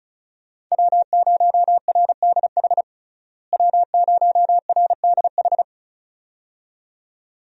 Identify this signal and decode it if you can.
Morse code, heard as a keyed tone, about 34 wpm, 700 Hz. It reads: W0RDH W0RDH
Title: USA Call Signs - 5 Characters - Repeated 2x - 35wpm